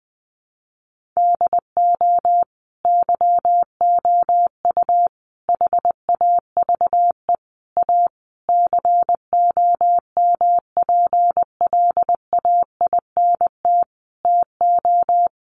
Morse code, heard as a keyed tone, about 20 wpm, 700 Hz. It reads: DO YOU HAVE A COMPLAINT TO
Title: Sentences from Top 1400 Words 20wpm